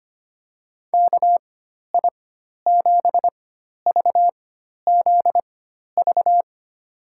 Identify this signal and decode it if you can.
Morse code, heard as a keyed tone, about 25 wpm, spaced as for 12 wpm, 700 Hz. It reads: KI7VZV
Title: USA Call Signs - 6 Characters - 4x Character Spacing - 25wpm